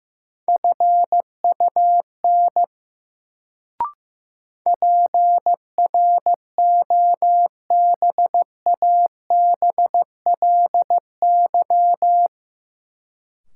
Morse code, heard as a keyed tone, about 15 wpm, 700 Hz. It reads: FUN PROBABLY
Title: Sentences from Top 700 Words - New Word Drill - 15wpm